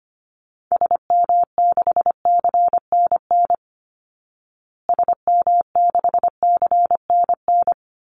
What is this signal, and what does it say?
Morse code, heard as a keyed tone, about 25 wpm, 700 Hz. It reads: SM6CNN SM6CNN
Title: Call Signs Repeated 2x - 25wpm